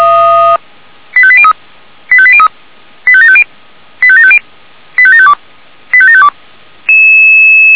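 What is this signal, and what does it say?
Five-tone selective call (ZVEI), heard as quick runs of five tones, tones of about 70 ms. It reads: 85803 85803 85750 85750 85752 85752
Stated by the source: ZVEI five-tone selective call 85726 received on SDR.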